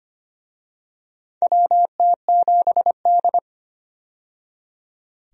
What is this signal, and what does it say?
Morse code, heard as a keyed tone, about 25 wpm, 700 Hz. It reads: WT7D
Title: USA Call Signs - 4 Characters - 25wpm